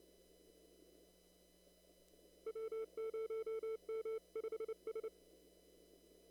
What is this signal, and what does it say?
Morse code, heard as a keyed tone, about 29 wpm, 440 Hz. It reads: W0M5S
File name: CW / kt5j wpx cw-op-k5pi